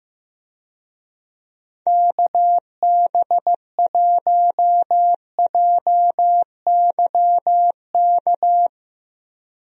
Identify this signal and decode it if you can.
Morse code, heard as a keyed tone, about 15 wpm, 700 Hz. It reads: KB1JYK
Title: USA Call Signs - 6 Characters - Repeated 2x - 15wpm